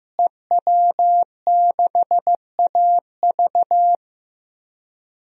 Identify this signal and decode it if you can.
Morse code, heard as a keyed tone, about 16 wpm, 700 Hz. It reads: EW6AV
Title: DXCC-100 Call Signs - 5 Characters - Repeated 2x - 15wpm